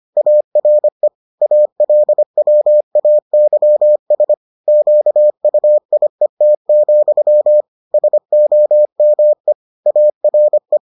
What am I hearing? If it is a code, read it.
Morse code, heard as a keyed tone, about 25 wpm, 600 Hz. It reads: ARE ALWAYS QUIET, SOME ARE